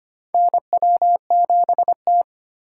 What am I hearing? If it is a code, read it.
Morse code, heard as a keyed tone, about 25 wpm, 700 Hz. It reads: NW7T
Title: USA Call Signs - 8x Character Spacing - 25wpm